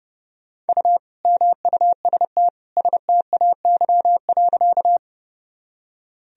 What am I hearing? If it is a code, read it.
Morse code, heard as a keyed tone, about 30 wpm, 700 Hz. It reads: U MUST STAY.
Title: Sentences from Top 500 Words and Abbreviations 30wpm